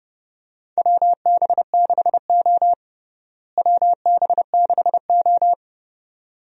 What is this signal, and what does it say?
Morse code, heard as a keyed tone, about 30 wpm, 700 Hz. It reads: WB6O WB6O
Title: USA Call Signs - 4 Characters - Repeated 2x - 30wpm